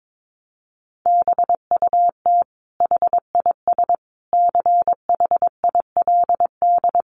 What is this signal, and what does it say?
Morse code, heard as a keyed tone, about 22 wpm, 700 Hz. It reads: BUT HIS CHILD
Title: Sentences from Top 800 Words 22wpm